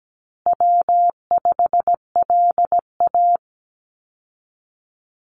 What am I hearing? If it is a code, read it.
Morse code, heard as a keyed tone, about 17 wpm, 700 Hz. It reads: W5LA
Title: Call Signs Repeated 3x - 17wpm